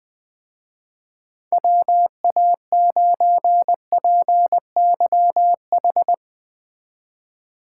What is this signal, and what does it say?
Morse code, heard as a keyed tone, about 19 wpm, 700 Hz. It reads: WA9PYH
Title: Speed Racing - USA 2x3 Call Signs 17wpm